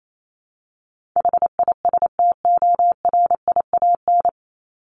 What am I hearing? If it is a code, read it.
Morse code, heard as a keyed tone, about 28 wpm, 700 Hz. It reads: HISTORIAN
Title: Top 1751-2000 Words 28wpm